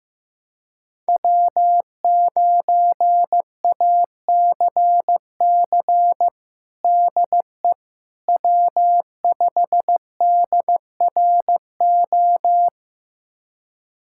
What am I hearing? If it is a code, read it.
Morse code, heard as a keyed tone, about 16 wpm, 700 Hz. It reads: W9ACC DE W5DRO
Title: USA Callsign DE Callsign - 5 Characters - 15wpm